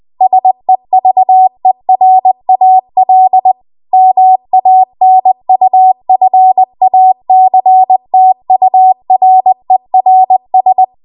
Morse code, heard as a keyed tone, about 20 wpm, 750 Hz. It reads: SEVERAL MANUFACTURERS